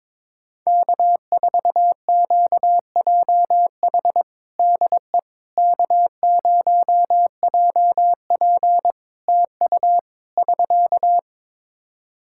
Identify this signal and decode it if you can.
Morse code, heard as a keyed tone, about 22 wpm, 700 Hz. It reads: K4QJH DE K0JP TU <SK>
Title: Practice QSO - Exchange 4 - Ending - 22wpm